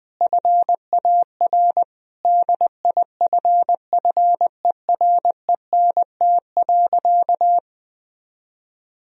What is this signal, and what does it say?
Morse code, heard as a keyed tone, about 20 wpm, 700 Hz. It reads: FAR DIFFERENT.
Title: Sentences from Top 1000 Words - Review Part 3 - 20wpm